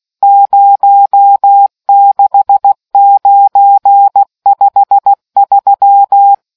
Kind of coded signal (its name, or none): Morse code